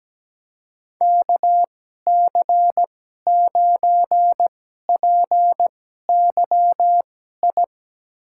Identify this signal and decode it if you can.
Morse code, heard as a keyed tone, about 17 wpm, spaced as for 12 wpm, 700 Hz. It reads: KC9PYI
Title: USA Call Signs - 6 Characters - 2x Character Spacing - 17wpm